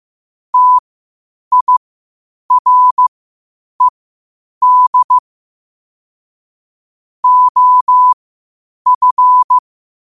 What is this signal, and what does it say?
Morse code, heard as a keyed tone, about 15 wpm, spaced as for 8 wpm, 1000 Hz. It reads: TIRED OF
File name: qotdfiles / 8wpm